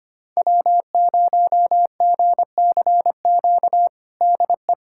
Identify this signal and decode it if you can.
Morse code, heard as a keyed tone, about 25 wpm, 700 Hz. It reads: W0GCQ DE
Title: USA Callsign DE Callsign 25wpm